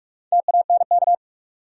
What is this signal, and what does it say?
Morse code, heard as a keyed tone, about 47 wpm, 700 Hz. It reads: TANK
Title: Four Letter Words - 45wpm